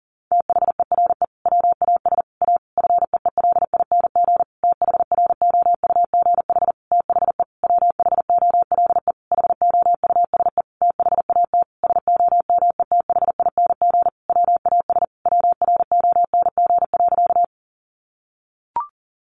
Morse code, heard as a keyed tone, about 40 wpm, 700 Hz. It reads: THERE WAS A FEELING THROUGH THE WHOLE HOUSE THAT SOMETHING WAS WRONG.
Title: Sentences from Top 1200 Words 40wpm